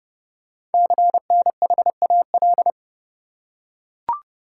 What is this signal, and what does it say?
Morse code, heard as a keyed tone, about 30 wpm, 690 Hz. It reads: CNHAL
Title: Code Groups - 5 Letters - 30wpm